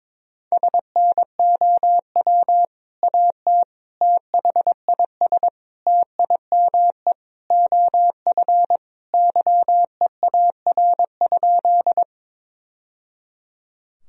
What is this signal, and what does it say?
Morse code, heard as a keyed tone, about 22 wpm, 700 Hz. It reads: SNOW AT THIS TIME OF YEAR?
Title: Sentences from Top 1000 Words - Review Part 3 - 22wpm